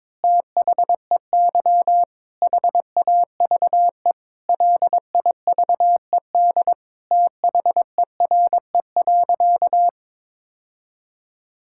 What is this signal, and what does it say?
Morse code, heard as a keyed tone, about 22 wpm, 700 Hz. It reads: THEY HAVE LIVED THERE.
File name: Sentences with 7-Word Phrases - 22wpm